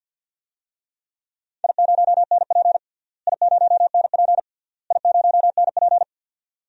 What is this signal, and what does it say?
Morse code, heard as a keyed tone, about 49 wpm, 700 Hz. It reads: I0NP I0NP I0NP
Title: DXCC-100 Call Signs - 4 Characters - Repeated 3x - 50wpm